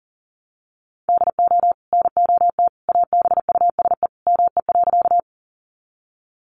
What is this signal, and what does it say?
Morse code, heard as a keyed tone, about 40 wpm, 700 Hz. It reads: DO NOT ABUSE ME.
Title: Sentences from Top 1300 Words 40wpm